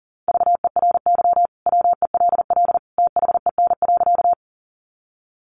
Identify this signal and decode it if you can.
Morse code, heard as a keyed tone, about 40 wpm, 700 Hz. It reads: VERY WELL THEN.
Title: Sentences from Top 200 Words 40wpm